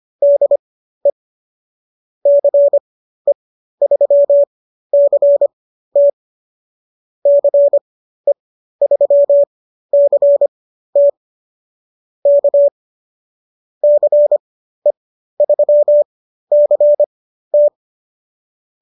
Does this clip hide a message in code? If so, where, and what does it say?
Yes: Morse code from 0.2 to 17.7 s, heard as a keyed tone, about 25 wpm, spaced as for 13 wpm, 590 Hz. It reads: DE CE3CT CE3CT K CE3CT